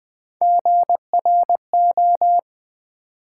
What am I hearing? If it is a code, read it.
Morse code, heard as a keyed tone, about 20 wpm, 700 Hz. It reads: GRO
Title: Build A Word - 1 Letter At A Time - 5-Letter Words - 20wpm